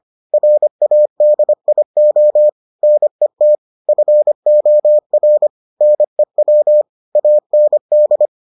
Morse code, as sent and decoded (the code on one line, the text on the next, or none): .-. .- -.. .. --- / -. - / ..-. --- .-. / -. .-- / .- -. -..
RADIO NET FOR NEW AND